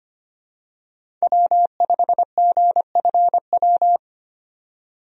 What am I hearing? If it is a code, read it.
Morse code, heard as a keyed tone, about 25 wpm, 700 Hz. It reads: W5GFW
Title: USA Call Signs - 5 Characters - 25wpm